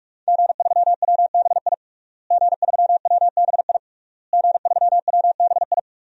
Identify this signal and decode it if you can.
Morse code, heard as a keyed tone, about 41 wpm, 700 Hz. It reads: G3WBI G3WBI G3WBI
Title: DXCC-100 Call Signs - 5 Characters - Repeated 3x - 45wpm